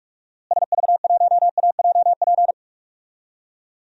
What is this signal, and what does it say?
Morse code, heard as a keyed tone, about 46 wpm, 700 Hz. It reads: SV1AJP
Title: DXCC-100 Call Signs - 6 Characters - Repeated 3x - 45wpm